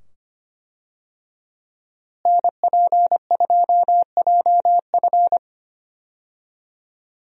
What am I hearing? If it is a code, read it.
Morse code, heard as a keyed tone, about 25 wpm, 700 Hz. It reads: NP2JF
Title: USA Call Signs - 5 Characters - Repeated 2x - 25wpm